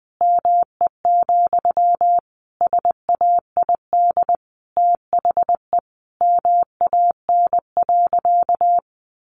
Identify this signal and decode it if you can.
Morse code, heard as a keyed tone, about 20 wpm, 700 Hz. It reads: ME, SAID THE MAN.
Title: Sentences from Top 200 Words 20wpm